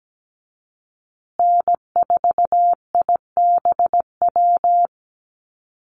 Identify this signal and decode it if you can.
Morse code, heard as a keyed tone, about 17 wpm, 700 Hz. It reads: N4IBW
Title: USA Call Signs Repeated 3x 17wpm